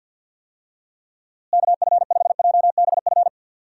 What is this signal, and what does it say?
Morse code, heard as a keyed tone, about 53 wpm, 700 Hz. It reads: KF5JBF
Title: USA Call Signs - 6 Characters - Repeated 2x - 50wpm